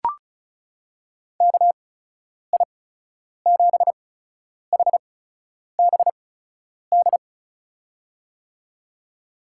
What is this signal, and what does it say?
Morse code, heard as a keyed tone, about 36 wpm, spaced as for 10 wpm, 700 Hz. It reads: KI7HBD
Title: USA Call Signs - 8x Character Spacing - 35wpm